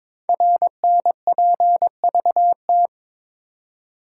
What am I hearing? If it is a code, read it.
Morse code, heard as a keyed tone, about 22 wpm, 700 Hz. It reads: RNPVT
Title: Code Groups - 5 Letters - 22wpm